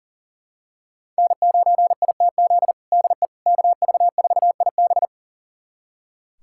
Morse code, heard as a keyed tone, about 40 wpm, 700 Hz. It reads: N9ITZ DE KV4IB
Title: USA Callsign DE Callsign - 5 Characters - 40wpm